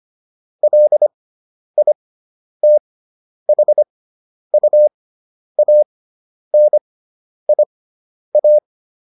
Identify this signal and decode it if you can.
Morse code, heard as a keyed tone, about 25 wpm, spaced as for 10 wpm, 600 Hz. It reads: LITHUANIA